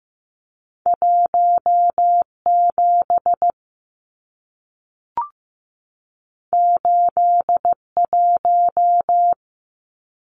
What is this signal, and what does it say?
Morse code, heard as a keyed tone, about 15 wpm, 700 Hz. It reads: 17 81
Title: Numbers - 2-Digit 15wpm